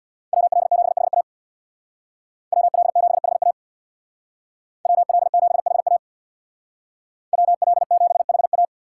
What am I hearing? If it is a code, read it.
Morse code, heard as a keyed tone, about 64 wpm, 700 Hz. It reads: WL7HA WL7HA WL7HA WL7HA